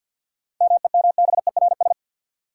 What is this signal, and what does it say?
Morse code, heard as a keyed tone, about 48 wpm, 700 Hz. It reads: MEMBERS
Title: Sets of 3 Words - New Word Drill - 50wpm